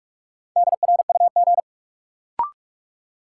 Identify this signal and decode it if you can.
Morse code, heard as a keyed tone, about 41 wpm, 700 Hz. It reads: DRUG
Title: Sets of 5 Words - New Word Drill - 45wpm